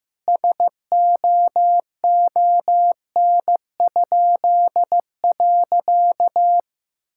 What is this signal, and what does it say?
Morse code, heard as a keyed tone, about 16 wpm, 700 Hz. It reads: SOON?.
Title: Pangram Challenge v2 - 15wpm